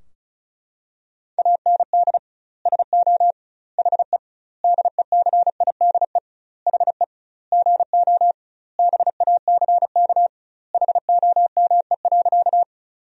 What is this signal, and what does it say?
Morse code, heard as a keyed tone, about 36 wpm, 700 Hz. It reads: AND SO HE DECIDE HE GO BACK HOME.
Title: 2x Word-Spacing - Sentences from Top 500 Words - 35wpm